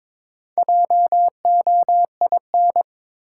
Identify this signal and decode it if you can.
Morse code, heard as a keyed tone, about 22 wpm, 700 Hz. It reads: JOIN
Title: Sentences from Top 1400 Words - New Word Drill Alt - 22wpm